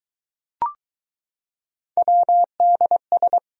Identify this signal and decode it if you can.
Morse code, heard as a keyed tone, about 20 wpm, 700 Hz. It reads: WDS